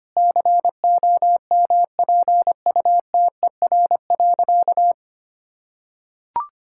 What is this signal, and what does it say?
Morse code, heard as a keyed tone, about 25 wpm, 700 Hz. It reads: COMPUTER.
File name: Easy Sentences with Hints - 25wpm